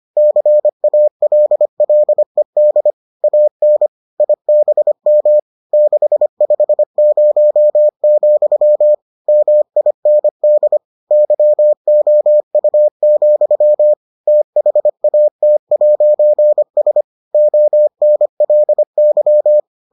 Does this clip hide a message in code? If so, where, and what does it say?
Yes: Morse code from 0.2 to 19.6 s, heard as a keyed tone, about 25 wpm, 600 Hz. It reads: CALLED AN IBM 650, MIND YOU, THAT'S ONLY